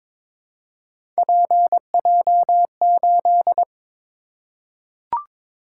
Morse code, heard as a keyed tone, about 23 wpm, 700 Hz. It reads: PJ8
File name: practice / 22wpm / Three Letter-Number Combinations - 22wpm